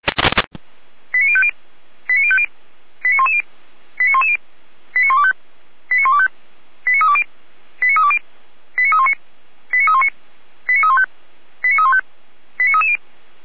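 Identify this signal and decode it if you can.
Five-tone selective call (ZVEI), heard as quick runs of five tones, tones of about 80 ms. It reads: 89050 89050 89110 89110 89125 89125 89320 89320 89319 89319 89315 89315 89330
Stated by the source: ZVEI five-tone selective call received by SDR.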